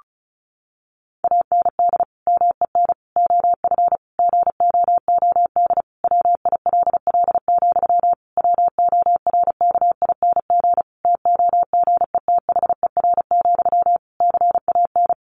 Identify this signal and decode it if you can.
Morse code, heard as a keyed tone, about 34 wpm, 700 Hz. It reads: AND MEN OF GOOD WILL, WORKING TOGETHER, CAN
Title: Sentences from Top 1400 Words 35wpm